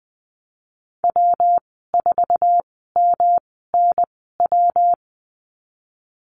Morse code, heard as a keyed tone, about 19 wpm, spaced as for 14 wpm, 700 Hz. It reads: W4MNW
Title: 128 Lesson 21g - Number 6 - Call Signs 20wpm